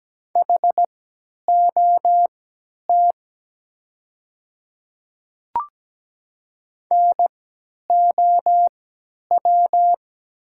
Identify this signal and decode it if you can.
Morse code, heard as a keyed tone, about 17 wpm, spaced as for 10 wpm, 700 Hz. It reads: HOT NOW
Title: 3x Character Spacing - Top 100 Words - 17wpm